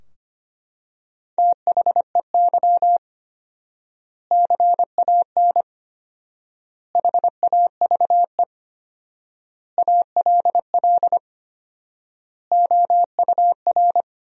Morse code, heard as a keyed tone, about 25 wpm, 700 Hz. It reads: THEY CAN HAVE ALL OUR
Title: 4x Word-Spacing - Sentences from Top 400 Words - 25wpm